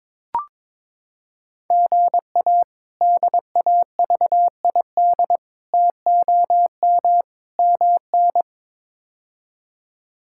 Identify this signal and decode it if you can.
Morse code, heard as a keyed tone, about 21 wpm, 700 Hz. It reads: GA DAVID TOM MN
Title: K1USN SST QSO Practice - 22wpm